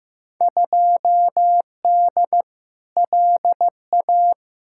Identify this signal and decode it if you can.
Morse code, heard as a keyed tone, about 15 wpm, 700 Hz. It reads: 2D LA
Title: Field Day Exchanges - 15wpm